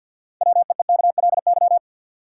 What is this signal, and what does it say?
Morse code, heard as a keyed tone, about 52 wpm, 700 Hz. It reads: WEEKLY
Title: Six Letter Words - 50wpm